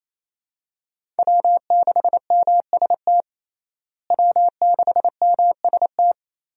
Morse code, heard as a keyed tone, about 28 wpm, 700 Hz. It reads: W6MST W6MST
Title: USA Call Signs - 5 Characters - Repeated 3x - 28wpm